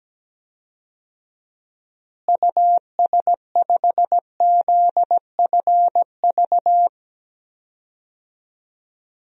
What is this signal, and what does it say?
Morse code, heard as a keyed tone, about 17 wpm, 700 Hz. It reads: US5ZFV
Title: DXCC-100 Call Signs - 6 Characters - Repeated 3x - 17wpm